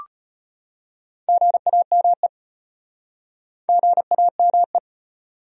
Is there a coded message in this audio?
Yes: Morse code from 1.3 to 4.8 s, heard as a keyed tone, about 38 wpm, 700 Hz. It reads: GAME GAME